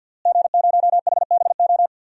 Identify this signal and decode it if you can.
Morse code, heard as a keyed tone, about 52 wpm, 700 Hz. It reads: G0HBO
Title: DXCC-100 Call Signs - 5 Characters - Repeated 3x - 50wpm